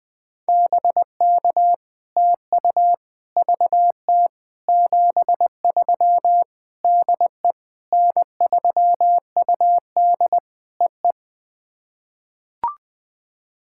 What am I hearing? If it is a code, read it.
Morse code, heard as a keyed tone, about 20 wpm, 700 Hz. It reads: BK TU VT 73 DE N3UD EE
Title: POTA QSO Practice - 20wpm